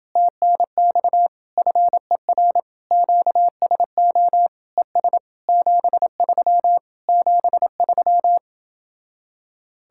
Morse code, heard as a keyed tone, about 28 wpm, 700 Hz. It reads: TNX FER QSO ES 73 73